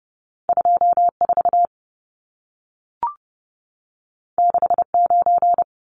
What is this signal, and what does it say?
Morse code, heard as a keyed tone, about 30 wpm, 700 Hz. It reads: 24 69
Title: Numbers - 2-Digit 30wpm